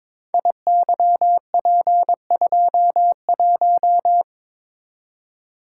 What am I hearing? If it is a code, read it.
Morse code, heard as a keyed tone, about 22 wpm, 700 Hz. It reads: IYP21
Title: Code Groups - 5 Letter:Numbers - 22wpm